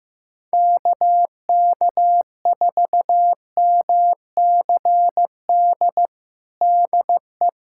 Morse code, heard as a keyed tone, about 14 wpm, 700 Hz. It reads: KK4MCD DE
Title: USA Callsign DE Callsign - 6 Characters - 15wpm